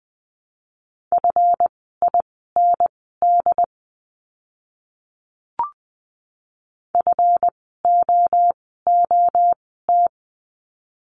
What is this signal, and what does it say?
Morse code, heard as a keyed tone, about 20 wpm, spaced as for 14 wpm, 700 Hz. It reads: FIND FOOT
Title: 139 Lesson 23f - Letter f - Words 20wpm